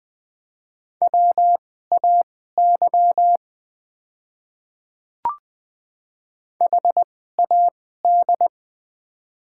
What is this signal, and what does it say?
Morse code, heard as a keyed tone, about 20 wpm, spaced as for 14 wpm, 700 Hz. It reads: WAY HAD
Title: 2x Character Spacing - Top 100 Words - 20wpm